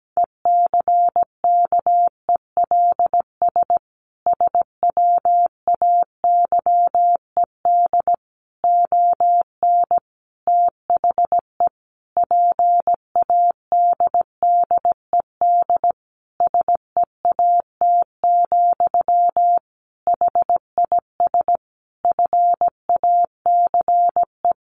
Morse code, heard as a keyed tone, about 17 wpm, 700 Hz. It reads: ECKELS SWAYED ON THE PADDED SEAT, HIS FACE
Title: A Sound Of Thunder 17wpm